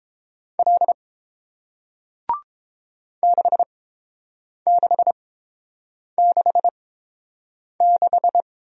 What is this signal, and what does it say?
Morse code, heard as a keyed tone, about 30 wpm, spaced as for 7 wpm, 700 Hz. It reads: L 6666